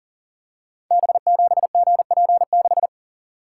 Morse code, heard as a keyed tone, about 38 wpm, 700 Hz. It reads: B7GP6
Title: Code Groups - 5 Letter:Numbers - 40wpm